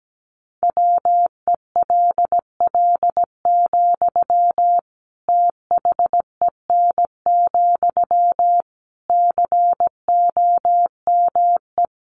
Morse code, heard as a keyed tone, about 17 wpm, 700 Hz. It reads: WELL, THEN, COME
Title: Sentences from Top 700 Words 17wpm